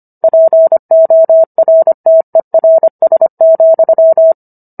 Morse code, heard as a keyed tone, about 25 wpm, 650 Hz. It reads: PORTERS,
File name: Book One / 16-Chapter 16 - 25 WPM